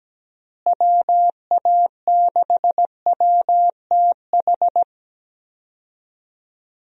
Morse code, heard as a keyed tone, about 17 wpm, 700 Hz. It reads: WA6WTH
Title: USA Call Signs - 6 Characters - Repeated 3x - 17wpm